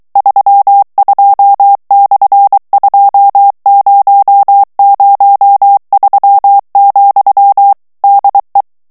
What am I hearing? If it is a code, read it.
Morse code, heard as a keyed tone, about 23 wpm, 800 Hz. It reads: 32/2003, DE
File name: morse mp3 / ESPAGNOL TEXTE EN CLAIR 2 VITESSE 22 MOTS MINUTES